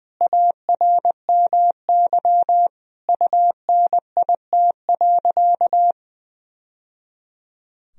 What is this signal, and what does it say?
Morse code, heard as a keyed tone, about 21 wpm, 700 Hz. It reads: ARMY UNIT.
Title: Sentences from Top 1000 Words - Review Part 2 - 20wpm